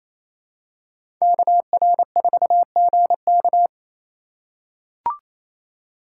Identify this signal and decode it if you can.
Morse code, heard as a keyed tone, about 27 wpm, 700 Hz. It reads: KR4GK
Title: USA Call Signs - 5 Characters - 2x Character Spacing - 28wpm